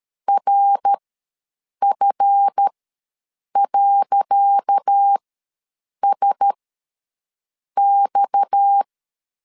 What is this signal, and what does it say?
Morse code, heard as a keyed tone, about 13 wpm, spaced as for 7 wpm, 800 Hz. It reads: RF.SX
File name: LEZIONI / 10 SIGNS / 10C DOT ws 12wpm 3548